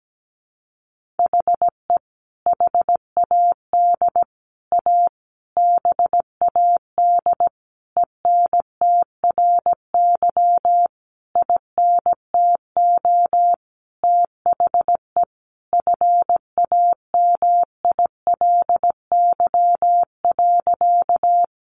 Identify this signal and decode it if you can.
Morse code, heard as a keyed tone, about 17 wpm, 700 Hz. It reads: HE HAD A BAD ENTRY INTO THE FAMILY.
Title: Sentences from Top 1500 Words 17wpm